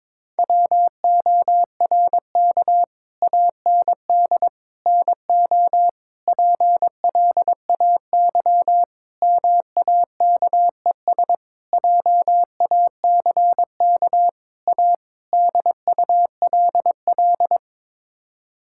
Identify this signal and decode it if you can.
Morse code, heard as a keyed tone, about 21 wpm, 700 Hz. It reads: WORK AND NO PLAY MAKES JACK A DULL
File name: practice / 22wpm / Sets of 8 Words - 22wpm